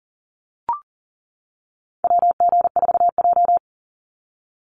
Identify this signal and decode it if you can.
Morse code, heard as a keyed tone, about 40 wpm, 700 Hz. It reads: WG4J
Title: USA Call Signs 40wpm